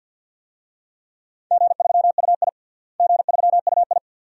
Morse code, heard as a keyed tone, about 52 wpm, 700 Hz. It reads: G3UI G3UI
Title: DXCC-100 Call Signs - 4 Characters - Repeated 2x - 50wpm